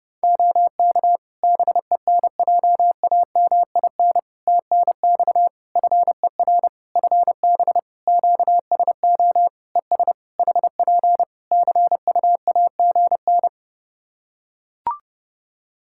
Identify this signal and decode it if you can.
Morse code, heard as a keyed tone, about 30 wpm, 700 Hz. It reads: OK BENJAMIN TNX FER FB QSO ES HP CUAGN
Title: Practice QSO - Exchange 4 - Ending - 30wpm